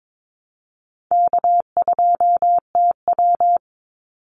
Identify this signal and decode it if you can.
Morse code, heard as a keyed tone, about 22 wpm, 700 Hz. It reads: K2TW
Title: Easy Call Signs Repeated 2x - 22wpm